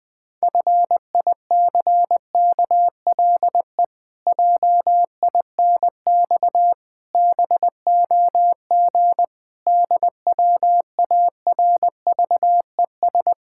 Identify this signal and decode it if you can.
Morse code, heard as a keyed tone, about 20 wpm, 700 Hz. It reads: FICKLE JINX BOG DWARVES
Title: Pangram Challenge v2 - 20wpm